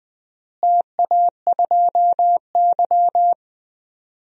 Morse code, heard as a keyed tone, about 21 wpm, 700 Hz. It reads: TA2Y
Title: DXCC-100 Call Signs - 4 Characters - Repeated 2x - 20wpm